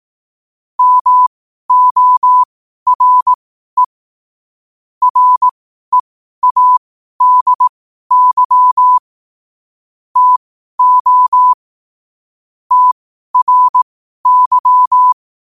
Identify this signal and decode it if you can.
Morse code, heard as a keyed tone, about 18 wpm, spaced as for 12 wpm, 1000 Hz. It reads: MORE READY TO TRY
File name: qotdfiles / 12wpm